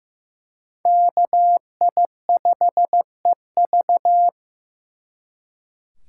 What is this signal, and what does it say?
Morse code, heard as a keyed tone, about 18 wpm, 700 Hz. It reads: KI5EV
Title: Speed Racing - USA 2x2 Call Signs 15wpm